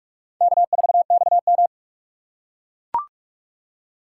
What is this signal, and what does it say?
Morse code, heard as a keyed tone, about 41 wpm, 700 Hz. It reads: K4XM
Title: USA Call Signs - 4 Characters - 4x Character Spacing - 45wpm